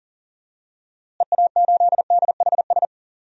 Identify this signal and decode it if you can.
Morse code, heard as a keyed tone, about 40 wpm, 700 Hz. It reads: EA8DHS
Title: DXCC-100 Call Signs - 6 Characters - Repeated 2x - 40wpm